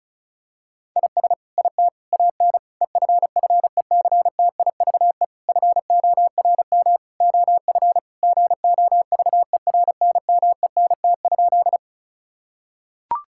Morse code, heard as a keyed tone, about 36 wpm, 700 Hz. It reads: IS IT AN EFFECTIVE FORM OF GOVERNMENT?
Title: Sentences from Top 1000 Words - Review Part 6 - 35wpm